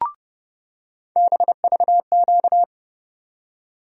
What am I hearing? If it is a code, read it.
Morse code, heard as a keyed tone, about 30 wpm, 700 Hz. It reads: BVQ
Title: USA Call Sign Postfixes - 3 Letters 30wpm